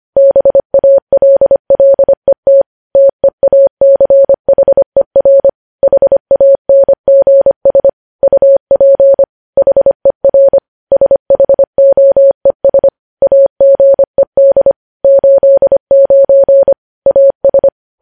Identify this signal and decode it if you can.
Morse code, heard as a keyed tone, about 25 wpm, 550 Hz. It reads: BALLET TEACHER HANGS UP HER SHOES AGED 89 AS
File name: podcast / News headlines-1x-25x25